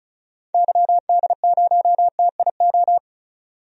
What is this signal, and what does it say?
Morse code, heard as a keyed tone, about 36 wpm, 700 Hz. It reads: YD0TIO
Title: DXCC-100 Call Signs - 6 Characters - 35wpm